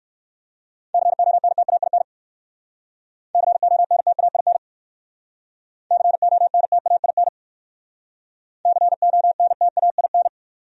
Morse code, heard as a keyed tone, about 57 wpm, 700 Hz. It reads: CONTAIN CONTAIN CONTAIN CONTAIN